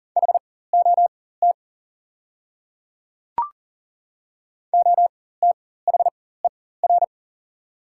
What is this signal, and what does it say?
Morse code, heard as a keyed tone, about 40 wpm, spaced as for 19 wpm, 700 Hz. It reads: HOT OTHER
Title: 4x Character Spacing - Top 100 Words - 40wpm